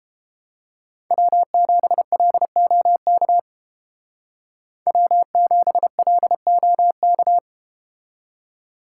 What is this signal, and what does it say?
Morse code, heard as a keyed tone, about 32 wpm, 700 Hz. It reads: W7LOK W7LOK